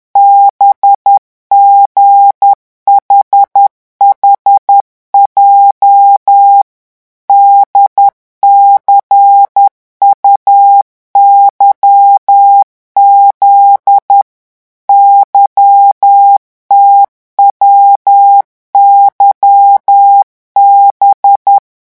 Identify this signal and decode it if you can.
Morse code, heard as a keyed tone, about 10 wpm, 800 Hz. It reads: BGHHJ DCUYZ YTWYB
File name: morse mp3 / revision11 lettres vitesse 10